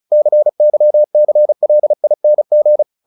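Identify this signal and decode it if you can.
Morse code, heard as a keyed tone, about 34 wpm, 600 Hz. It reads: CYCLING
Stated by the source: Morse practice at 35 wpm from CW med Gnister, sending Ken Follett's novel Never in English.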